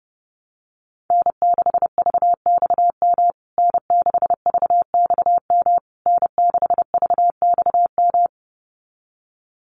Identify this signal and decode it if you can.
Morse code, heard as a keyed tone, about 30 wpm, 700 Hz. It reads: N6VXM N6VXM N6VXM
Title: USA Call Signs Repeated 3x 30wpm